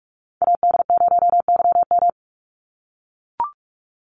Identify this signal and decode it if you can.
Morse code, heard as a keyed tone, about 44 wpm, 700 Hz. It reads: AD0YM
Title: NA SOTA Chaser Callsigns 45wpm